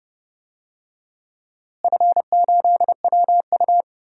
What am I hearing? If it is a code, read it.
Morse code, heard as a keyed tone, about 29 wpm, 700 Hz. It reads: F8WU
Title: DXCC-100 Call Signs - 4 Characters - Repeated 3x - 30wpm